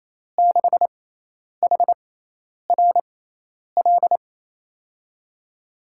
Morse code, heard as a keyed tone, about 28 wpm, spaced as for 10 wpm, 700 Hz. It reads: 6HRL
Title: USA Call Signs - 6 Characters - 6x Character Spacing - 28wpm